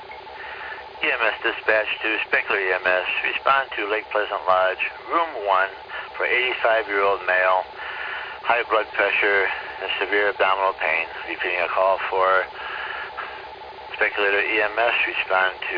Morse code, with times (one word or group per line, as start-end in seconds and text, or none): none